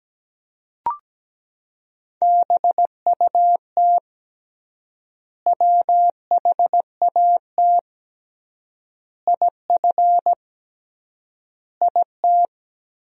Morse code, heard as a keyed tone, about 17 wpm, 700 Hz. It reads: BUT WHAT IF IT
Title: 3x Word-Spacing - Sentences from Top 100 Words - 17wpm